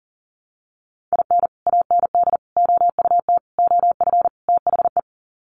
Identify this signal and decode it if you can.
Morse code, heard as a keyed tone, about 40 wpm, 700 Hz. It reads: IN AND OUT OF THE
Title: Sets of 5 Words 40wpm